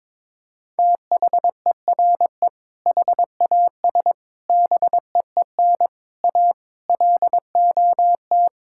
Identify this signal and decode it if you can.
Morse code, heard as a keyed tone, about 22 wpm, 700 Hz. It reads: THERE HAS BEEN A LOT
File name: practice / 22wpm / Sets of 8 Words - 22wpm